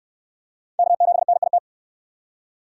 Morse code, heard as a keyed tone, about 68 wpm, 700 Hz. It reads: K7NIT